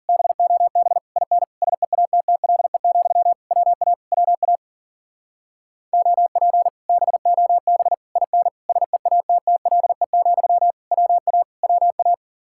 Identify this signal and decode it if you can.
Morse code, heard as a keyed tone, about 48 wpm, 700 Hz. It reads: BOB IN SEATTLE, WA WA OP BOB IN SEATTLE, WA WA
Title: Speed-Racing Common QSO Phrases Encore - 40wpm